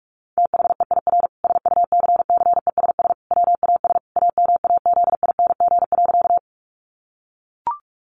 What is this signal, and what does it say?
Morse code, heard as a keyed tone, about 44 wpm, 700 Hz. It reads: THEIR SUCCESS WAS AMAZING.
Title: Sentences from Top 1500 Words 45wpm